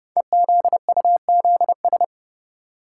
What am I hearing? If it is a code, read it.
Morse code, heard as a keyed tone, about 28 wpm, 700 Hz. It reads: EZUZS
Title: Code Groups - 5 Letters - 30wpm